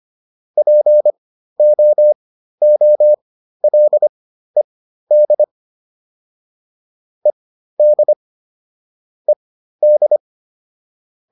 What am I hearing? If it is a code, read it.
Morse code, heard as a keyed tone, about 25 wpm, spaced as for 13 wpm, 600 Hz. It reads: POOLED ED ED